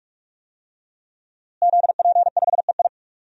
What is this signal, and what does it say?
Morse code, heard as a keyed tone, about 46 wpm, 700 Hz. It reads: ZP5EI
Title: DXCC-100 Call Signs - 5 Characters - Repeated 2x - 45wpm